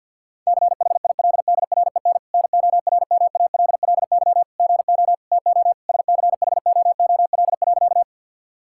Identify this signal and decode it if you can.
Morse code, heard as a keyed tone, about 50 wpm, 700 Hz. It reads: CHILDREN NORMALLY GO TO SCHOOL.
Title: Sentences from Top 1000 Words - Review Part 6 - 50wpm